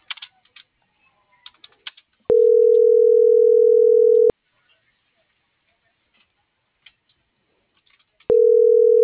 Ringback tone (North America), 440 + 480 Hz.
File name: CENTRAL MABE / out antonio.garcia - .56868